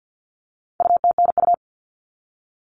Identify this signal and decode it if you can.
Morse code, heard as a keyed tone, about 50 wpm, 700 Hz. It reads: UTNU
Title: All Cut Numbers - 4-Digit 50wpm